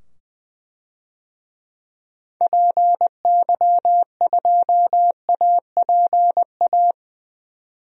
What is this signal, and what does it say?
Morse code, heard as a keyed tone, about 20 wpm, 700 Hz. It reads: PY2APA
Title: DXCC-100 Call Signs - 6 Characters - 20wpm